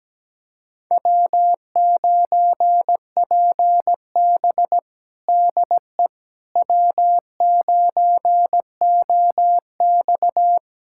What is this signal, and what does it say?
Morse code, heard as a keyed tone, about 17 wpm, 700 Hz. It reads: W9PB DE W9OX
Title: USA Callsign DE Callsign - 4 Characters - 17wpm